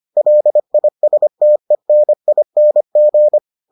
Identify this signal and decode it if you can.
Morse code, heard as a keyed tone, about 25 wpm, 600 Hz. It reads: LISTENING